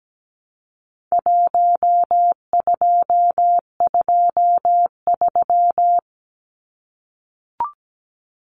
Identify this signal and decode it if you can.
Morse code, heard as a keyed tone, about 17 wpm, 700 Hz. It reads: 1223
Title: Numbers - Contest Style - 4-Digit 17wpm